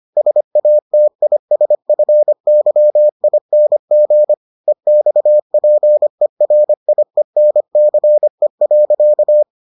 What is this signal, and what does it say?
Morse code, heard as a keyed tone, about 25 wpm, 600 Hz. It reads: SATISFYING EXPERIENCE.